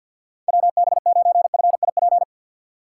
Morse code, heard as a keyed tone, about 52 wpm, 700 Hz. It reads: WB9FIP
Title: K1USN SST QSO Practice - 50wpm